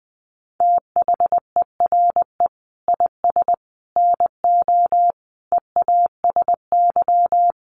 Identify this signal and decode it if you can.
Morse code, heard as a keyed tone, about 20 wpm, 700 Hz. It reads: THERE IS NO EASY
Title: Sentences from Top 1300 Words 20wpm